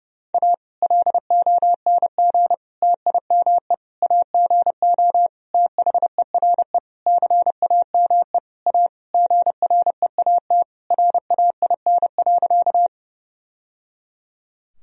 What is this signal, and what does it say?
Morse code, heard as a keyed tone, about 31 wpm, 700 Hz. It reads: A LONG TIME AGO THERE CAME A GREAT RAIN.
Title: Sentences from Top 1000 Words - Review Part 3 - 30wpm